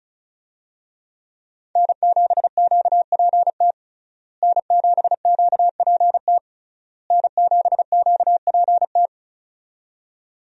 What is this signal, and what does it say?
Morse code, heard as a keyed tone, about 36 wpm, 700 Hz. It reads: N7QPT N7QPT N7QPT
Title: USA Call Signs - 5 Characters - Repeated 3x - 35wpm